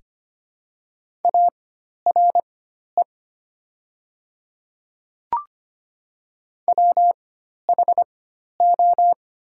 Morse code, heard as a keyed tone, about 25 wpm, spaced as for 12 wpm, 690 Hz. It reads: ARE WHO
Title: 4x Character Spacing - Top 300 Words - 25wpm